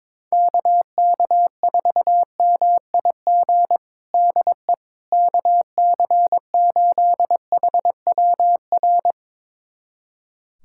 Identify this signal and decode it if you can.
Morse code, heard as a keyed tone, about 21 wpm, 700 Hz. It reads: KK4MIG DE KC8HWR
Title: USA Callsign DE Callsign 22wpm